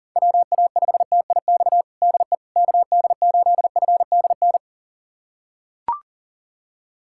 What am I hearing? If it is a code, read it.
Morse code, heard as a keyed tone, about 40 wpm, 700 Hz. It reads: WA5TIX DE KD8FDN
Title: USA Callsign DE Callsign - 6 Characters - 40wpm